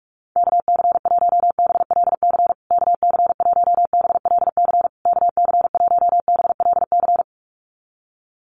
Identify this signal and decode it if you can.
Morse code, heard as a keyed tone, about 44 wpm, 700 Hz. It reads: KC1BLC KC1BLC KC1BLC
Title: USA Call Signs Repeated 3x 45wpm